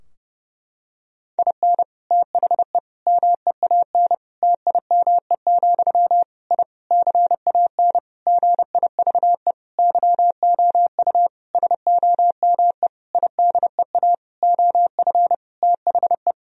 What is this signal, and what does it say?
Morse code, heard as a keyed tone, about 30 wpm, 700 Hz. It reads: IN THE MEAN TIME, I CAN GIVE YOU SOME IDEA OF THE
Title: Sentences from Top 1000 Words - Review Part 3 - 30wpm